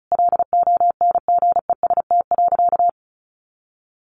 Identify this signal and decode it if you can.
Morse code, heard as a keyed tone, about 34 wpm, 700 Hz. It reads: LONGEST.
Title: A Sound Of Thunder 35wpm